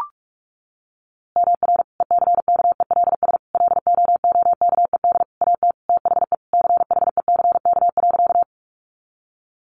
Morse code, heard as a keyed tone, about 44 wpm, 700 Hz. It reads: MR ECKELS LOOKED AT THE CHECK.
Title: A Sound Of Thunder 45wpm